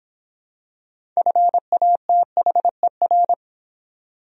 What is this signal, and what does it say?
Morse code, heard as a keyed tone, about 24 wpm, 700 Hz. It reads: FATHER